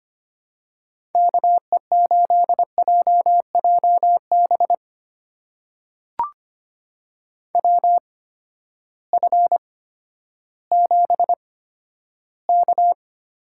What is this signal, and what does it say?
Morse code, heard as a keyed tone, about 25 wpm, spaced as for 12 wpm, 700 Hz. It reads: KE8JJB WF7K
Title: USA Call Signs - 8x Character Spacing - 25wpm